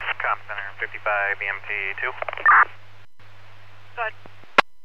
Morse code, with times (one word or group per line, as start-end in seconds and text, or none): none